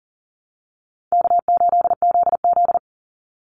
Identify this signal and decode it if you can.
Morse code, heard as a keyed tone, about 40 wpm, 700 Hz. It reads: K8ZZ
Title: Easy Call Signs Repeated 2x - 40wpm